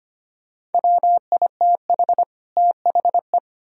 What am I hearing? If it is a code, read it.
Morse code, heard as a keyed tone, about 33 wpm, 700 Hz. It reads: WITH THE